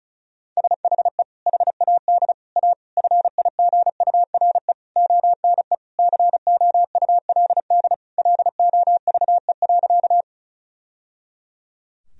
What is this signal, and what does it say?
Morse code, heard as a keyed tone, about 36 wpm, 700 Hz. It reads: SHE HAD A FIGURE ONE COULD LOVE.
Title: Sentences from Top 1000 Words - Review Part 2 - 35wpm